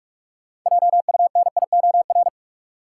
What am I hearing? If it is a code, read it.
Morse code, heard as a keyed tone, about 41 wpm, 700 Hz. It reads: JUNIOR
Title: Sentences from Top 1000 Words - Review Part 7 - New Word Drill - 45wpm